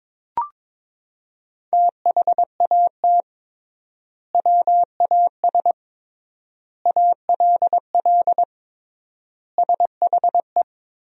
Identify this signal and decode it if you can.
Morse code, heard as a keyed tone, about 22 wpm, 700 Hz. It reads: THAT WAS ALL SHE
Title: 3x Word-Spacing - Sentences from Top 100 Words - 22wpm